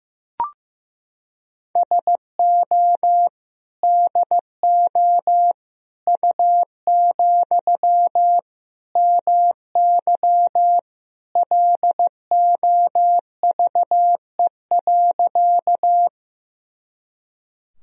Morse code, heard as a keyed tone, about 15 wpm, 700 Hz. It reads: SO DO U, MY LOVE.
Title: Sentences from Top 400 Words and Abbreviations 15wpm